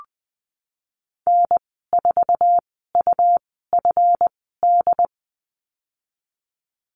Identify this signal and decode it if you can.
Morse code, heard as a keyed tone, about 19 wpm, spaced as for 14 wpm, 700 Hz. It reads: N4UFD
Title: 167 Lesson 27g - Number 7 - Call Signs 20wpm